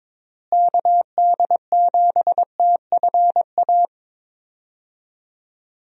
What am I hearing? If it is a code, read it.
Morse code, heard as a keyed tone, about 22 wpm, 700 Hz. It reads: KD7TFA
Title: USA Call Signs - 6 Characters - Repeated 3x - 22wpm